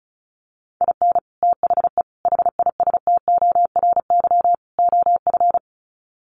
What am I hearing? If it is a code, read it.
Morse code, heard as a keyed tone, about 34 wpm, 700 Hz. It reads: IN THE HISTORY OF
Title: Sets of 4 Words 35wpm